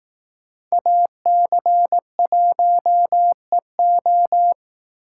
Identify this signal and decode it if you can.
Morse code, heard as a keyed tone, about 18 wpm, 700 Hz. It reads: AC1EO
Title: Speed Racing - USA 2x2 Call Signs 15wpm